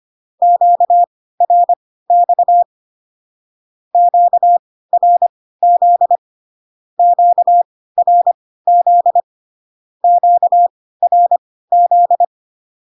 Morse code, heard as a keyed tone, about 25 wpm, spaced as for 16 wpm, 700 Hz. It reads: QRX QRZ QRZ QRZ
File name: cw / Q-codes most common repeated 3 times 10 25wpm ik1hge